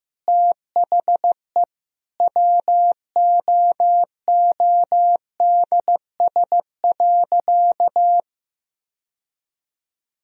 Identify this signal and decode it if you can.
Morse code, heard as a keyed tone, about 15 wpm, 700 Hz. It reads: THE WOODS.
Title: Sentences from Top 1000 Words - Review Part 2 - 15wpm